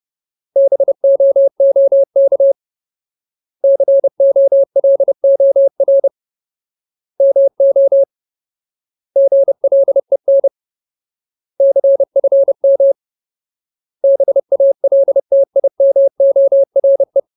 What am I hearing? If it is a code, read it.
Morse code, heard as a keyed tone, about 30 wpm, 550 Hz. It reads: BOOK COLOR MO GLEN CFM BALTIMORE
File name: Ultimate Word List / 30WPM Ultimate